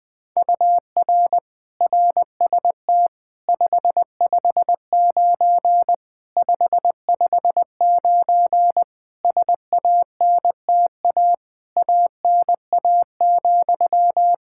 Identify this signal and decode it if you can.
Morse code, heard as a keyed tone, about 20 wpm, 700 Hz. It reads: UR RST 559 559 SANTA ANA,
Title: Practice QSO - Exchange 1 - Beginning - 20wpm